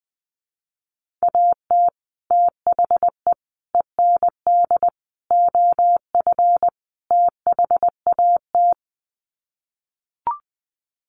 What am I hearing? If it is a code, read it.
Morse code, heard as a keyed tone, about 20 wpm, 700 Hz. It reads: AT THE END OF THAT
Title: Sets of 5 Words Encore2 20wpm